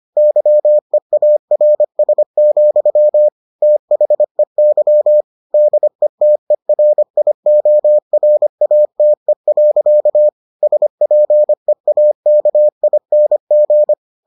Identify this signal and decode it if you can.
Morse code, heard as a keyed tone, about 25 wpm, 600 Hz. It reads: YEARS, THEY DETERIORATE. SPEAKING